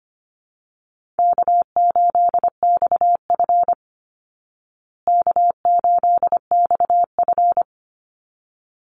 Morse code, heard as a keyed tone, about 25 wpm, 700 Hz. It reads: K8XF K8XF
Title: Call Signs Repeated 3x - 25wpm